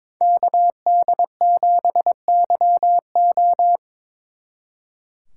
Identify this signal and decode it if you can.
Morse code, heard as a keyed tone, about 25 wpm, 700 Hz. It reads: KD7YO
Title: Speed Racing - USA 2x2 Call Signs 22wpm